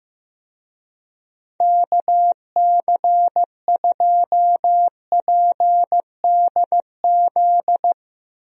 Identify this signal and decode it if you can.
Morse code, heard as a keyed tone, about 15 wpm, 700 Hz. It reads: KC2PDZ
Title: USA Call Signs - 6 Characters - Repeated 3x - 15wpm